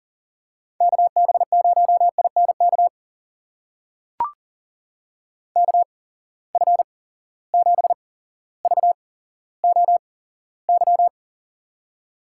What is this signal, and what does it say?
Morse code, heard as a keyed tone, about 40 wpm, spaced as for 17 wpm, 700 Hz. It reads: KB0INK KF7VOY
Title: USA Call Signs - 6 Characters - 8x Character Spacing - 40wpm